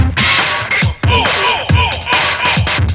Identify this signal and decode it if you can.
There is no code.